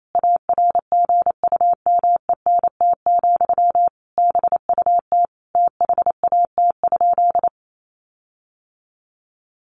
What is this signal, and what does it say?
Morse code, heard as a keyed tone, about 28 wpm, 700 Hz. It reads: ARGUMENT, BUT THAT?
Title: Sentences from Top 1400 Words 28wpm